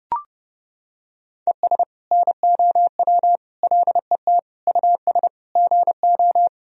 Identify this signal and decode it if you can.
Morse code, heard as a keyed tone, about 30 wpm, 700 Hz. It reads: ES NOW LET US GO
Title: Sentences from Top 300 Words and Abbreviations 30wpm